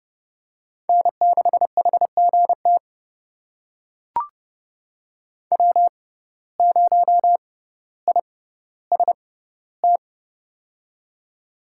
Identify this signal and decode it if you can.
Morse code, heard as a keyed tone, about 30 wpm, spaced as for 15 wpm, 690 Hz. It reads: N6HGT W0IST
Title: USA Call Signs - 5 Characters - 6x Character Spacing - 30wpm